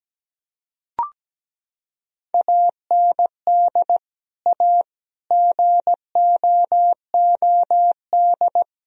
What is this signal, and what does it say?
Morse code, heard as a keyed tone, about 17 wpm, 700 Hz. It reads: AND A GOOD
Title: Sentences from Top 1000 Words - Review Part 3 - 17wpm